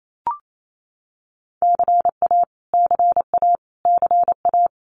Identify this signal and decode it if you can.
Morse code, heard as a keyed tone, about 28 wpm, 700 Hz. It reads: CA CA CA
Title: US States Repeated 3x 28wpm